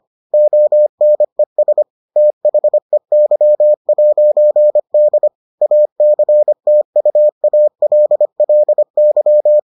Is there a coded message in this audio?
Yes: Morse code from 0.3 to 9.6 s, heard as a keyed tone, about 25 wpm, 600 Hz. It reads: ONES THEY'D ACTUALLY